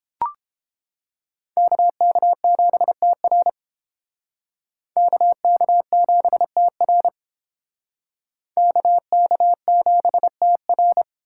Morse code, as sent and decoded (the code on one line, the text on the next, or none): -.- -.- --... - .-. / -.- -.- --... - .-. / -.- -.- --... - .-.
KK7TR KK7TR KK7TR